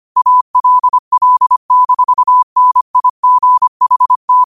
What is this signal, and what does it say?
Morse code, heard as a keyed tone, about 25 wpm, 1000 Hz. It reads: ALL-NIGHT